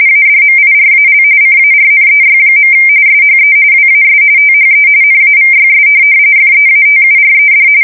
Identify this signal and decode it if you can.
RTTY, about 45.45 baud. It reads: ADMIRAL CARNEY. SHE WAS THE NAVYS FEATURED
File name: fsk170 / USS-New-Jersey